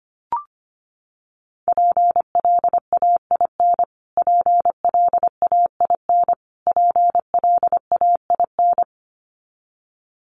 Morse code, heard as a keyed tone, about 25 wpm, 700 Hz. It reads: PLAIN PLAIN PLAIN
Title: Top 500 Words Repeated 25wpm